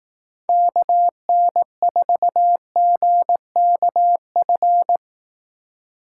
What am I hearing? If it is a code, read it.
Morse code, heard as a keyed tone, about 18 wpm, 700 Hz. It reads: KN4GKF
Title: Speed Racing - USA 2x3 Call Signs 15wpm